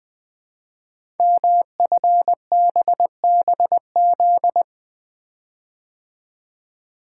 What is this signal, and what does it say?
Morse code, heard as a keyed tone, about 20 wpm, 700 Hz. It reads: MFBBZ
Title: Code Groups - 5 Letters - 20wpm